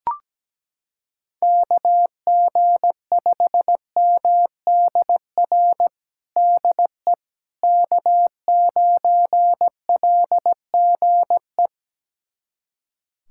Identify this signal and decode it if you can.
Morse code, heard as a keyed tone, about 17 wpm, 700 Hz. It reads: KG5MDR DE K9LGE
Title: USA Callsign DE Callsign 17wpm